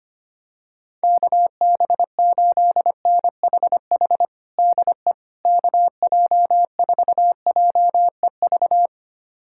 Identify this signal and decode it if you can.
Morse code, heard as a keyed tone, about 25 wpm, 700 Hz. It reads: KB8NHH DE KJ4JEV
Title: USA Callsign DE Callsign 25wpm